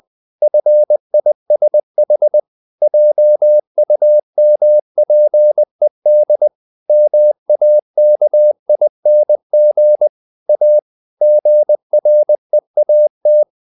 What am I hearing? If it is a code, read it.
Morse code, heard as a keyed tone, about 20 wpm, 600 Hz. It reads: FISH JUMPED MAKING A GREAT
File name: the old man and the sea part 2 20wpm